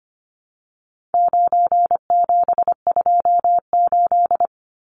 Morse code, heard as a keyed tone, about 25 wpm, 700 Hz. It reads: 9728
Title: Numbers - 4-Digit 25wpm